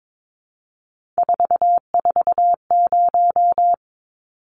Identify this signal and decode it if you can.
Morse code, heard as a keyed tone, about 22 wpm, 700 Hz. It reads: 440
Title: Numbers - 3-Digit 22wpm